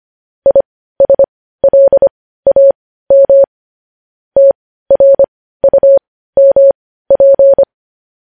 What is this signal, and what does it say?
Morse code, heard as a keyed tone, about 25 wpm, spaced as for 15 wpm, 550 Hz. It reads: ISLAM TRUMP
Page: news-headlines-in-morse-code-at-15-wpm